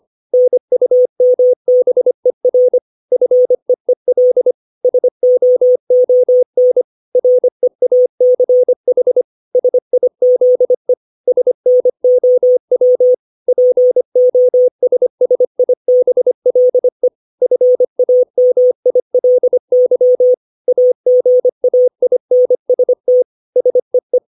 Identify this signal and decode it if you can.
Morse code, heard as a keyed tone, about 25 wpm, 500 Hz. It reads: NUMBER FEEL SOON REACH SIZE SNOW POSSIBLE FAMILY AGAINST SEE
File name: 632 words / 632 Words 25WPM